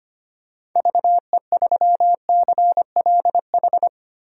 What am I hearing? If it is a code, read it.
Morse code, heard as a keyed tone, about 25 wpm, 690 Hz. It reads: VE3CLH
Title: DXCC-100 Call Signs - 6 Characters - 25wpm